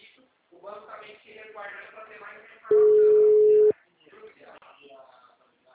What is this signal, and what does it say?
Ringback tone (Europe, 425 Hz), with speech.
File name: out- -2009- - - .2